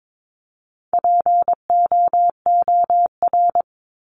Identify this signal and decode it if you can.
Morse code, heard as a keyed tone, about 22 wpm, 700 Hz. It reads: POOR
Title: Top 751-1000 Words 22wpm